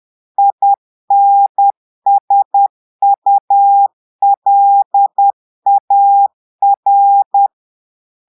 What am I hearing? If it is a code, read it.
Morse code, heard as a keyed tone, about 10 wpm, 800 Hz. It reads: INSULAR
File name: Portugais - texte en clair 1 - vitesse - 10 mots minute